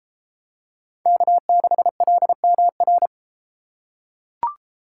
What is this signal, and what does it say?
Morse code, heard as a keyed tone, about 30 wpm, 700 Hz. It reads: K6LMR